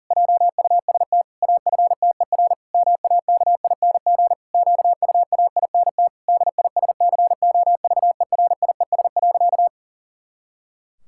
Morse code, heard as a keyed tone, about 40 wpm, 700 Hz. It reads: JUST AFTER MAKING QUAINT DISCOVERIES.
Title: Pangram Challenge v2 - 40wpm